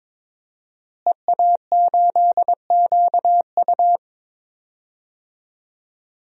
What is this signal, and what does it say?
Morse code, heard as a keyed tone, about 22 wpm, 700 Hz. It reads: EA8QU
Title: DXCC-100 Call Signs - 5 Characters - Repeated 2x - 22wpm